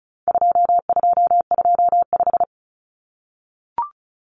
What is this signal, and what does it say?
Morse code, heard as a keyed tone, about 34 wpm, 700 Hz. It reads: 2225
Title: Common Cut Numbers - 4-Digit 35wpm